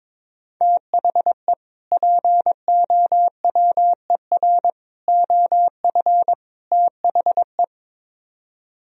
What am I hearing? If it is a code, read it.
Morse code, heard as a keyed tone, about 22 wpm, 700 Hz. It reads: THE POWER OF THE
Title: Speed-Racing - Sets of 4 Words 22wpm